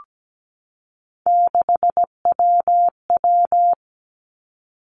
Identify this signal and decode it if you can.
Morse code, heard as a keyed tone, about 17 wpm, 700 Hz. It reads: 6WW
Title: Three Challenging Letter-Number Combinations 17wpm